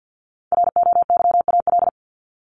Morse code, heard as a keyed tone, about 50 wpm, 700 Hz. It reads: ROYAL
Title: Top 1751-2000 Words 50wpm